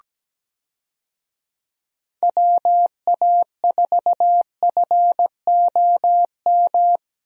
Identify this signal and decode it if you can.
Morse code, heard as a keyed tone, about 17 wpm, 700 Hz. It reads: WA4FOM
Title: USA Call Signs - 6 Characters - Repeated 3x - 17wpm